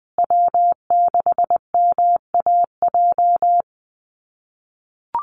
Morse code, heard as a keyed tone, about 20 wpm, 700 Hz. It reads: W6MAJ
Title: USA Call Signs Repeated 3x 20wpm